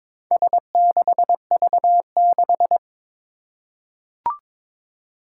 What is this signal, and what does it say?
Morse code, heard as a keyed tone, about 22 wpm, 700 Hz. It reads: S6V6
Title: Four Challenging Letter-Number Combinations 22wpm